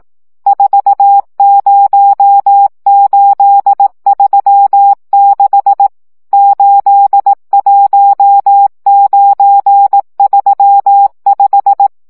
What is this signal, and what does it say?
Morse code, heard as a keyed tone, about 18 wpm, 800 Hz. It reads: 40836 81935
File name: morze vizsga 2011-3 15